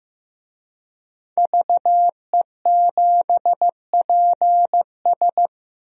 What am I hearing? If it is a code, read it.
Morse code, heard as a keyed tone, about 16 wpm, 700 Hz. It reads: VE7PS
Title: DXCC-100 Call Signs - 5 Characters - 15wpm